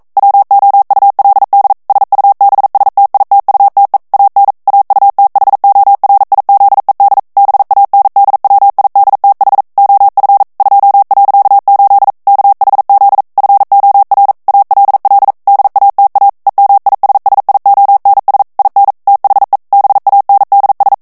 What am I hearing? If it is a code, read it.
Morse code, heard as a keyed tone, about 42 wpm, 800 Hz. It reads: WOULD SUBSTITUTE AN AUTHORIZED BANDWIDTH OF 2.8 KHZ FOR ALL DATA EMISSIONS IN THE BANDS
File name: 40 mots minute